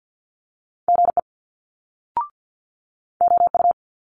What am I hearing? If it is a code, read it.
Morse code, heard as a keyed tone, about 50 wpm, 700 Hz. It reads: GE OU
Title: Multi-letter Phonemes Quick Fire 50wpm